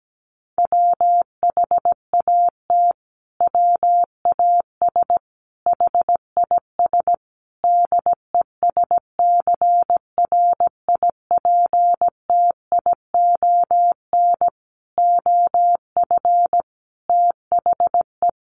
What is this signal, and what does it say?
Morse code, heard as a keyed tone, about 17 wpm, 700 Hz. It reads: WHAT WAS HIS DESCRIPTION OF THE
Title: Sentences from Top 1300 Words 17wpm